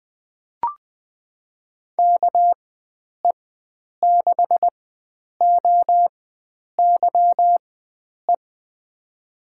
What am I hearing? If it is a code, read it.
Morse code, heard as a keyed tone, about 20 wpm, spaced as for 9 wpm, 700 Hz. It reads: KE6OYE
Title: USA Call Signs - 6 Characters - 4x Character Spacing - 20wpm